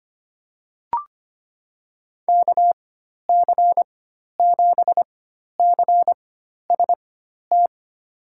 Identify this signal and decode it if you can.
Morse code, heard as a keyed tone, about 25 wpm, spaced as for 12 wpm, 700 Hz. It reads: KC7CST
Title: USA Call Signs - 6 Characters - 4x Character Spacing - 25wpm